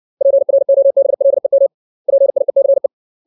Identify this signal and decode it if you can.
Morse code, heard as a keyed tone, about 60 wpm, 550 Hz. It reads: PROBLEM PIECE
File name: 632 words / 632 Words 60WPM